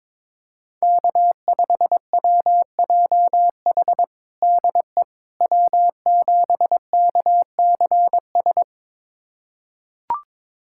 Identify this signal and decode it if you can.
Morse code, heard as a keyed tone, about 22 wpm, 700 Hz. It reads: K5WJH DE W7KCS
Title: USA Callsign DE Callsign - 5 Characters - 22wpm